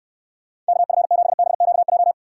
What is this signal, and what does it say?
Morse code, heard as a keyed tone, about 69 wpm, 700 Hz. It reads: KF7BQJ